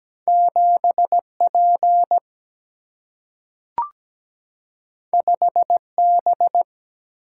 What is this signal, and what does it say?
Morse code, heard as a keyed tone, about 17 wpm, 700 Hz. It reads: 7P 5B
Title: Single-Letter Single-Number 17wpm